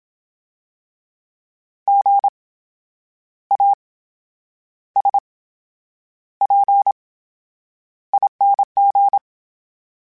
Morse code, heard as a keyed tone, about 27 wpm, spaced as for 9 wpm, 800 Hz. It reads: GASPING
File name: Special INGs / Special INGs-mp3s / VerySlowToFaster Ing @25